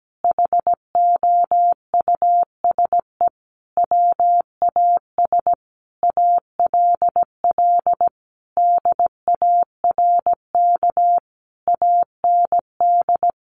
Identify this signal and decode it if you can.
Morse code, heard as a keyed tone, about 17 wpm, 700 Hz. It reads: HOUSE WAS ALL DARK AND
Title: Sentences from Top 500 Words 17wpm